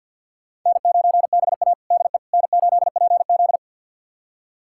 Morse code, heard as a keyed tone, about 46 wpm, 700 Hz. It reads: N9BA DE N8PZ
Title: USA Callsign DE Callsign - 4 Characters - 50wpm